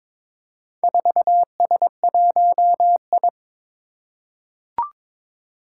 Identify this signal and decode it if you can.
Morse code, heard as a keyed tone, about 22 wpm, 700 Hz. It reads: 4S1I
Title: Four Challenging Letter-Number Combinations 22wpm